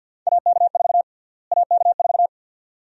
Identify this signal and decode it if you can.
Morse code, heard as a keyed tone, about 51 wpm, 700 Hz. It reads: AK4 AK4
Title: USA Call Sign Prefixes - Repeated 2x - 50wpm